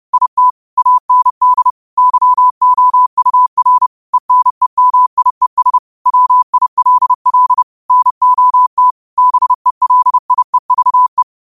Morse code, heard as a keyed tone, about 30 wpm, 1000 Hz. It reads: IT AND YOUR ENEMIES WILL NOT BELIEVE
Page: New quotes every day in morse code at 30 Words per minute.